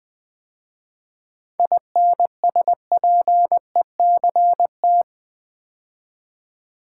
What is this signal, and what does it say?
Morse code, heard as a keyed tone, about 20 wpm, 690 Hz. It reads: INSPECT
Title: 7 Letter Words - First 3 Letters Given - 20wpm